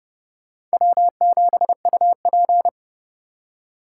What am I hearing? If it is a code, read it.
Morse code, heard as a keyed tone, about 30 wpm, 700 Hz. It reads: W7UP